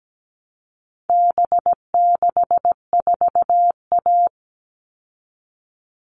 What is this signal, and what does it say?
Morse code, heard as a keyed tone, about 17 wpm, 700 Hz. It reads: B64A
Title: All Cut Numbers - 4-Digit 17wpm